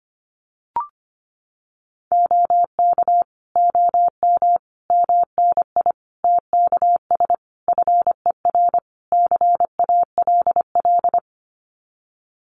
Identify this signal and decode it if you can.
Morse code, heard as a keyed tone, about 25 wpm, 700 Hz. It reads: OK OM MNI TKS FER CALL
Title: Common QSO Phrases 25wpm